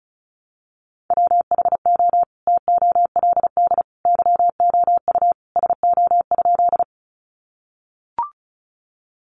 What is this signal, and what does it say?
Morse code, heard as a keyed tone, about 34 wpm, 700 Hz. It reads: WHO TOLD YOU SO?
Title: Sentences from Top 400 Words 35wpm